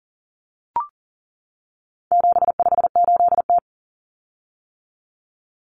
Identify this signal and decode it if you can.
Morse code, heard as a keyed tone, about 40 wpm, 700 Hz. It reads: 758T
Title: Common Cut Numbers - 4-Digit 40wpm